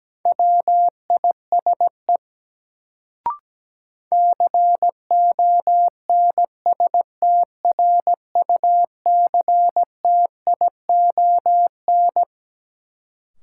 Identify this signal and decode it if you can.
Morse code, heard as a keyed tone, about 17 wpm, 700 Hz. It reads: WISE CONSTRUCTION
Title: Sentences from Top 1600 Words - New Word Drill Alt - 17wpm